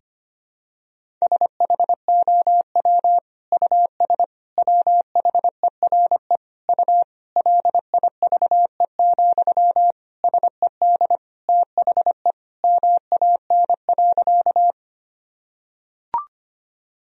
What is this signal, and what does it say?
Morse code, heard as a keyed tone, about 25 wpm, 700 Hz. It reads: SHOW US WHERE U LIVE, SED THE MAN.
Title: Sentences from Top 200 Words and Abbreviations 25wpm